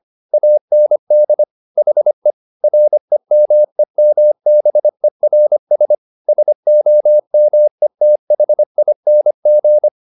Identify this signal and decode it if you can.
Morse code, heard as a keyed tone, about 25 wpm, 600 Hz. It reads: AND HE REMEMBERS SOMETHING
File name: the old man and the sea part 1 25wpm